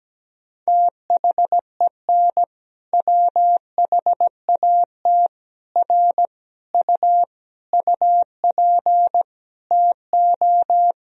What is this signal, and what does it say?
Morse code, heard as a keyed tone, about 17 wpm, 700 Hz. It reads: THEN WHAT R U UP TO
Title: Sentences from Top 100 Words and Abbreviations 17wpm